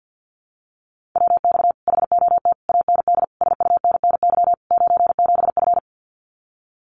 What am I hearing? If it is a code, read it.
Morse code, heard as a keyed tone, about 50 wpm, 700 Hz. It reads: WX HOT AND SUNNY 97F
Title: Common QSO Phrases Encore 50wpm